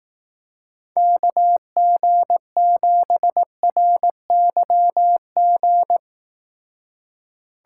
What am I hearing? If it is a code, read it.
Morse code, heard as a keyed tone, about 18 wpm, 700 Hz. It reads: KG7RYG
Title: Speed Racing - USA Call Signs 15wpm